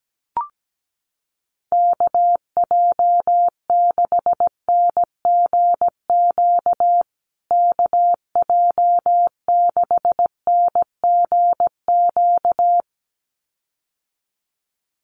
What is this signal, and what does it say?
Morse code, heard as a keyed tone, about 17 wpm, 700 Hz. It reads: KJ6NGQ KJ6NGQ
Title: USA Call Signs Repeated 2x 17wpm